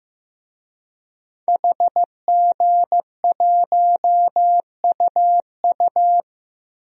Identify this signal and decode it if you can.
Morse code, heard as a keyed tone, about 15 wpm, 700 Hz. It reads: HG1UU
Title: DXCC-100 Call Signs - 5 Characters - 15wpm